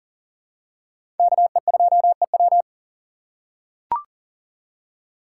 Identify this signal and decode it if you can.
Morse code, heard as a keyed tone, about 40 wpm, 700 Hz. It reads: KE2EW
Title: USA Call Signs - 5 Characters - 2x Character Spacing - 40wpm